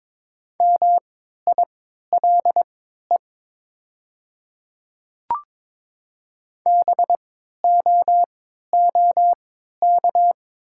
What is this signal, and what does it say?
Morse code, heard as a keyed tone, about 22 wpm, spaced as for 13 wpm, 700 Hz. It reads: MILE BOOK
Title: 3x Character Spacing - Top 500 Words - 22wpm